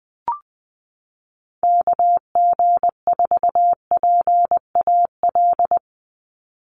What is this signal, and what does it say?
Morse code, heard as a keyed tone, about 20 wpm, 700 Hz. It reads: KG4PAL
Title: USA Call Signs 20wpm